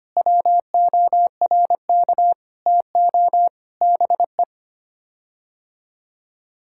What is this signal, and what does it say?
Morse code, heard as a keyed tone, about 25 wpm, 700 Hz. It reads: WORK TO BE
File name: Sets of 9 Words - 25wpm